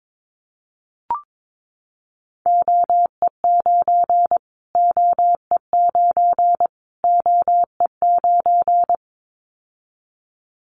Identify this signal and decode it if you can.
Morse code, heard as a keyed tone, about 22 wpm, 700 Hz. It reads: OE9 OE9 OE9
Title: Callsign Prefixes for Top 25 DXCC - Repeated 3x - 22wpm